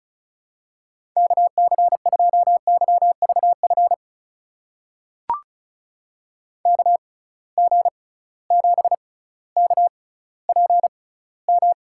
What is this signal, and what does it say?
Morse code, heard as a keyed tone, about 36 wpm, spaced as for 18 wpm, 700 Hz. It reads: KC2YVF KG7KPM
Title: USA Call Signs - 6 Characters - 6x Character Spacing - 35wpm